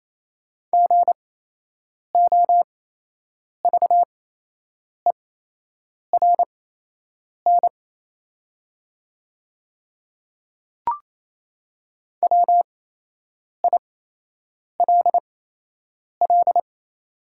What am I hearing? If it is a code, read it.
Morse code, heard as a keyed tone, about 28 wpm, spaced as for 8 wpm, 700 Hz. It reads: GOVERN WILL
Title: 8x Character Spacing - Top 500 Words - 28wpm